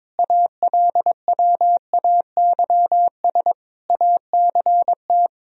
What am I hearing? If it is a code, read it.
Morse code, heard as a keyed tone, about 22 wpm, 700 Hz. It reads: ALWAYS ACT
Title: Sentences from Top 1000 Words - Review Part 1 - 22wpm